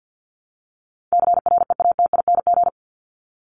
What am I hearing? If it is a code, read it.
Morse code, heard as a keyed tone, about 50 wpm, 700 Hz. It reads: CREATING
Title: ING Suffix - Top 100 Words with Base Words 50wpm